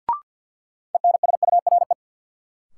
Morse code, heard as a keyed tone, about 52 wpm, 700 Hz. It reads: ENSURE
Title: Sentences from Top 1100 Words - New Word Drill - 50wpm